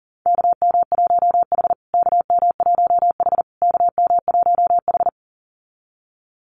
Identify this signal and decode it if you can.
Morse code, heard as a keyed tone, about 40 wpm, 700 Hz. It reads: KM1H KM1H KM1H
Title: USA Call Signs Repeated 3x 40wpm